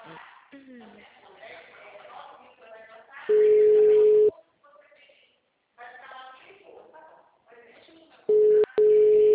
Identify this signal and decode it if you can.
Ringback tone (Europe, 425 Hz), with speech.